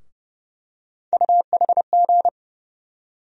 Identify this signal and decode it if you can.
Morse code, heard as a keyed tone, about 32 wpm, 700 Hz. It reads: UHG
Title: USA Call Sign Postfixes - 3 Letters 30wpm